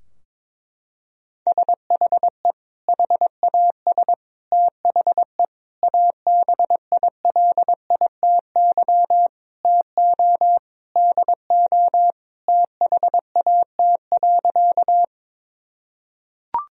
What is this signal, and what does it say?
Morse code, heard as a keyed tone, about 22 wpm, 700 Hz. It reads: SHE HAS THE ABILITY TO DO THAT.
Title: Sentences from Top 1000 Words - Review Part 3 - 22wpm